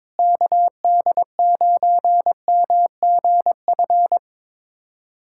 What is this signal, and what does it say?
Morse code, heard as a keyed tone, about 22 wpm, 700 Hz. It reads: KD9MGF
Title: USA Callsign DE Callsign - 6 Characters - 22wpm